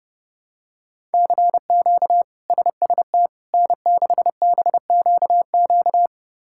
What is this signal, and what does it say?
Morse code, heard as a keyed tone, about 30 wpm, 700 Hz. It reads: CQ SST N6BQQ
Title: K1USN SST QSO Practice - 30wpm